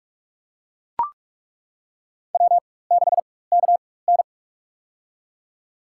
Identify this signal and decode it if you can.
Morse code, heard as a keyed tone, about 41 wpm, spaced as for 20 wpm, 700 Hz. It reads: W6KN
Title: USA Call Signs - 4 Characters - 4x Character Spacing - 45wpm